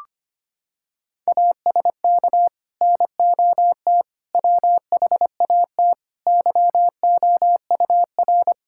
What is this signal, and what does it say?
Morse code, heard as a keyed tone, about 25 wpm, 700 Hz. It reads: ASK NOT WHAT YOUR
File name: practice / 25wpm / Sentences with 9-Word Phrases - 25wpm